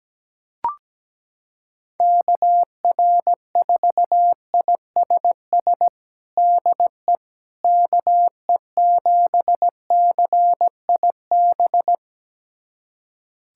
Morse code, heard as a keyed tone, about 17 wpm, 700 Hz. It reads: KR4ISS DE KE7CIB
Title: USA Callsign DE Callsign - 6 Characters - 17wpm